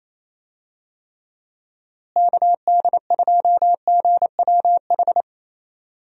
Morse code, heard as a keyed tone, about 28 wpm, 700 Hz. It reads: KD2GWH
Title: USA Call Signs - 6 Characters - Repeated 3x - 28wpm